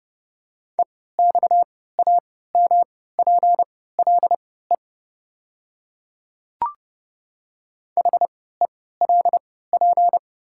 Morse code, heard as a keyed tone, about 30 wpm, spaced as for 17 wpm, 690 Hz. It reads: EXAMPLE HELP
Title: 3x Character Spacing - Top 300 Words - 30wpm